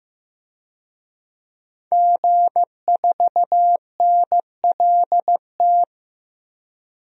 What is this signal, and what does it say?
Morse code, heard as a keyed tone, about 15 wpm, 700 Hz. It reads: G4NLT
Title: DXCC-100 Call Signs - 5 Characters - 15wpm